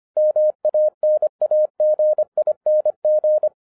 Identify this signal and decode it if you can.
Morse code, heard as a keyed tone, about 25 wpm, 610 Hz. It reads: MANAGING
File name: 2011-10-mp3 / n04-JUST MOVED 25wpm